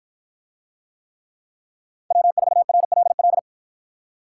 Morse code, heard as a keyed tone, about 53 wpm, 700 Hz. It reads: W4RLL